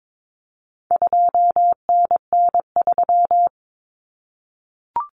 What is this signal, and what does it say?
Morse code, heard as a keyed tone, about 22 wpm, 700 Hz. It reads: 2NN3
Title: Common Cut Numbers - 4-Digit 22wpm